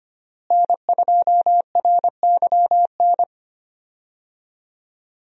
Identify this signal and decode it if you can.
Morse code, heard as a keyed tone, about 25 wpm, 700 Hz. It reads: N2RYN
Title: USA Call Signs - 5 Characters - Repeated 3x - 25wpm